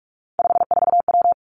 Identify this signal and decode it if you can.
Morse code, heard as a keyed tone, about 46 wpm, 700 Hz. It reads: 54W
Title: Three Challenging Letter-Number Combinations 45wpm